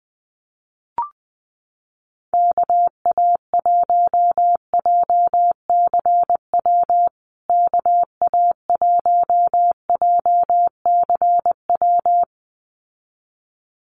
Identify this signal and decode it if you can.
Morse code, heard as a keyed tone, about 20 wpm, 700 Hz. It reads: KA1JCW KA1JCW
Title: USA Call Signs Repeated 2x 20wpm